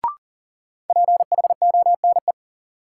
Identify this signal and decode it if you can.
Morse code, heard as a keyed tone, about 37 wpm, 700 Hz. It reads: PHONE
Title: Sentences from Top 1100 Words - New Word Drill - 40wpm